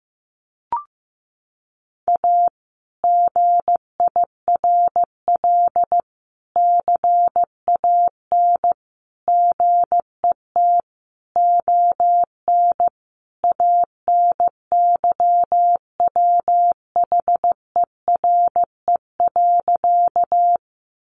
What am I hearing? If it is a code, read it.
Morse code, heard as a keyed tone, about 15 wpm, 700 Hz. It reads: A GIRL CAN GET ON ANYWHERE.
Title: Sentences from Top 1400 Words 15wpm